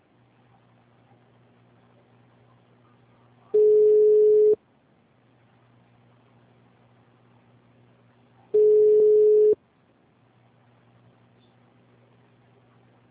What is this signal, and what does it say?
Ringback tone (Europe, 425 Hz).